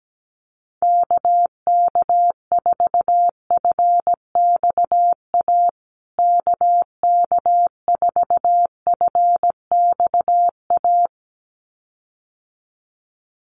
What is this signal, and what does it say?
Morse code, heard as a keyed tone, about 17 wpm, 700 Hz. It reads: KK4FXA KK4FXA
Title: USA Call Signs Repeated 2x 17wpm